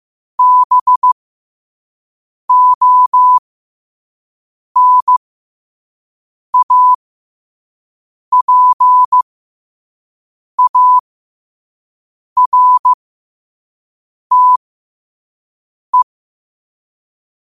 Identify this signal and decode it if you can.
Morse code, heard as a keyed tone, about 15 wpm, spaced as for 5 wpm, 1000 Hz. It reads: BONAPARTE
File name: qotdfiles / 5wpm